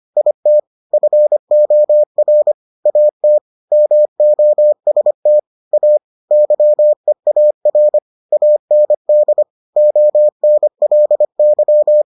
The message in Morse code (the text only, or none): IT FOR AT MOST A YEAR AND ONLY